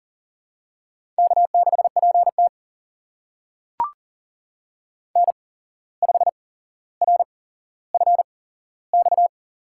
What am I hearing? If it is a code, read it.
Morse code, heard as a keyed tone, about 40 wpm, spaced as for 16 wpm, 700 Hz. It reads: K6PT N5RFX
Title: USA Call Signs - 8x Character Spacing - 40wpm